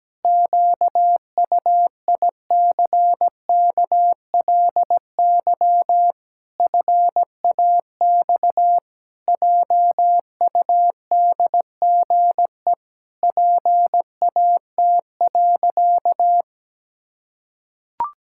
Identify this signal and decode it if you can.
Morse code, heard as a keyed tone, about 17 wpm, 700 Hz. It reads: QUICKLY FAX JUDGE PAT.
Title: Pangram Challenge v2 - 17wpm